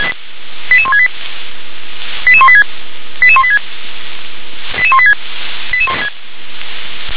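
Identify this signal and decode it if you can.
Five-tone selective call (ZVEI), heard as quick runs of five tones, tones of about 70 ms. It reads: 88168 88186 88186 88186
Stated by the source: ZVEI five-tone selective call received by SDR.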